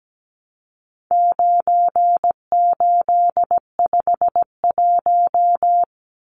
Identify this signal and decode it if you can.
Morse code, heard as a keyed tone, about 17 wpm, 700 Hz. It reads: 9851
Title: Numbers - 4-Digit 17wpm